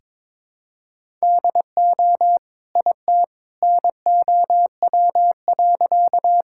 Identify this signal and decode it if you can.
Morse code, heard as a keyed tone, about 22 wpm, 700 Hz. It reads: DO IT NOW.
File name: practice / 22wpm / Sentences with 2-Word Phrases - 22wpm